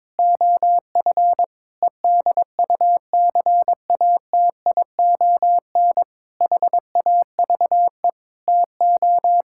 Morse code, heard as a keyed tone, about 22 wpm, 700 Hz. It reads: OF EDUCATION HAVE TO
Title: Sentences from Top 1000 Words - Review Part 6 - 22wpm